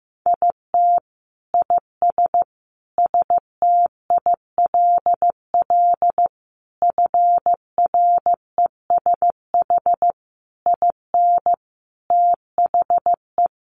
Morse code, heard as a keyed tone, about 15 wpm, 700 Hz. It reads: IT IS STILL FRESH IN THE
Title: Sentences from Top 1300 Words 15wpm